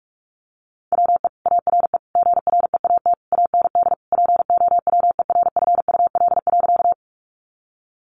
Morse code, heard as a keyed tone, about 44 wpm, 700 Hz. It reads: WE ARE GREAT AND POWERFUL.
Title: Sentences from Top 1300 Words 45wpm